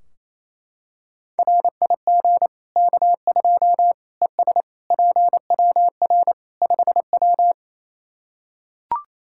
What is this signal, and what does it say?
Morse code, heard as a keyed tone, about 28 wpm, 700 Hz. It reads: RIG K2 ES PWR 5W
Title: Practice QSO - Exchange 2 - Info - 28wpm